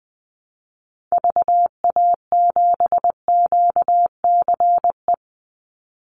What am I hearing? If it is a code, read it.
Morse code, heard as a keyed tone, about 19 wpm, 700 Hz. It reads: VA7QCE
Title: Call Signs Repeated 3x - 20wpm